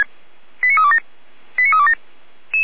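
Five-tone selective call (ZVEI), heard as quick runs of five tones, tones of about 70 ms. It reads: 89327 89327
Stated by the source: ZVEI five-tone selective call, received by SDR.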